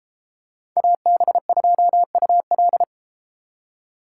Morse code, heard as a keyed tone, about 32 wpm, 700 Hz. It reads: AB2UL